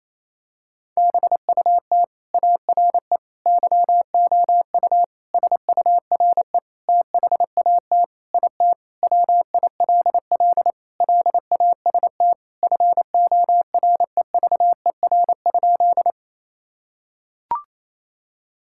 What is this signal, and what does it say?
Morse code, heard as a keyed tone, about 27 wpm, 700 Hz. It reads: BUT ARE YOU SURE THAT IT WILL LAST FOREVER?
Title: Sentences from Top 1600 Words 28wpm